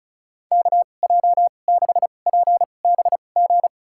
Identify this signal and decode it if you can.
Morse code, heard as a keyed tone, about 34 wpm, spaced as for 25 wpm, 700 Hz. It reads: KJ6PBG
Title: USA Call Signs - 2x Character Spacing - 35wpm